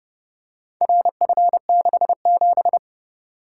Morse code, heard as a keyed tone, about 30 wpm, 700 Hz. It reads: RF67